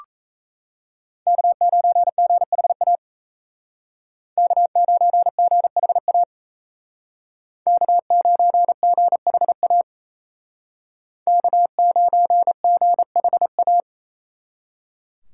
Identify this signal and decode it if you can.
Morse code, heard as a keyed tone, about 36 wpm, 700 Hz. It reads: K9GHA K9GHA K9GHA K9GHA